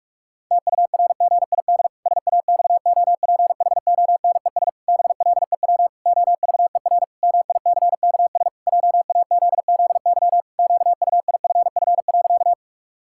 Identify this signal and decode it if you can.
Morse code, heard as a keyed tone, about 41 wpm, 700 Hz. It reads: TURGID SAXOPHONES BLEW OVER MICKS JAZZY QUIFF.
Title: Pangram Challenge v2 - 45wpm